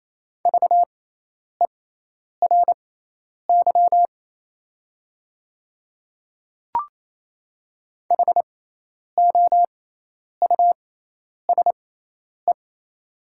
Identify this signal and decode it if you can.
Morse code, heard as a keyed tone, about 28 wpm, spaced as for 10 wpm, 700 Hz. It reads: VERY HOUSE
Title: 6x Character Spacing - Top 300 Words - 28wpm